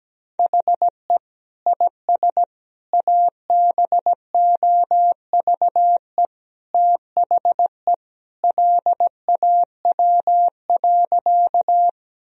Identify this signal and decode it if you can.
Morse code, heard as a keyed tone, about 17 wpm, 700 Hz. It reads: HE IS ABOVE THE LAW.
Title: Sentences from Top 1000 Words - Review Part 3 - 17wpm